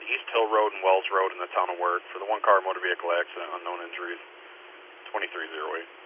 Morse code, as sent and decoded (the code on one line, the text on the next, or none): none